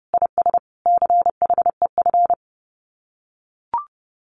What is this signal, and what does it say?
Morse code, heard as a keyed tone, about 30 wpm, 700 Hz. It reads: IS CHEF
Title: Common QSO Phrases Encore 30wpm